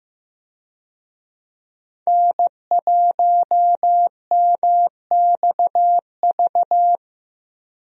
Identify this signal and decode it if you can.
Morse code, heard as a keyed tone, about 15 wpm, 700 Hz. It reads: N1MXV
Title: USA Call Signs - 5 Characters - Repeated 3x - 15wpm